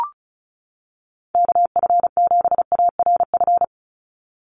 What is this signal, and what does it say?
Morse code, heard as a keyed tone, about 34 wpm, 700 Hz. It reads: KF7ARF
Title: USA Call Signs 35wpm